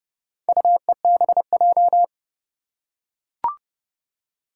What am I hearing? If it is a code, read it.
Morse code, heard as a keyed tone, about 30 wpm, 700 Hz. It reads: UEBJ
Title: Four Challenging Letter-Number Combinations 30wpm